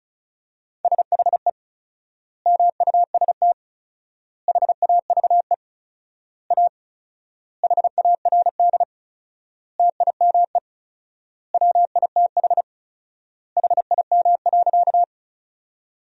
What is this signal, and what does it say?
Morse code, heard as a keyed tone, about 34 wpm, 700 Hz. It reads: SHE MUST HAVE A HARD TIME WITH HIM.
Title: 4x Word-Spacing - Sentences from Top 300 Words - 35wpm